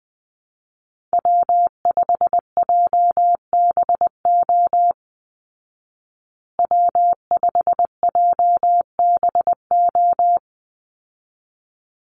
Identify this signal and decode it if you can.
Morse code, heard as a keyed tone, about 19 wpm, 700 Hz. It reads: W5JBO W5JBO
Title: Call Signs Repeated 3x - 20wpm